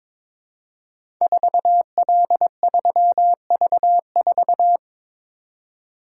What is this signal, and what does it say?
Morse code, heard as a keyed tone, about 22 wpm, 700 Hz. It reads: 4L3V4
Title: Code Groups - 5 Letter:Numbers - 22wpm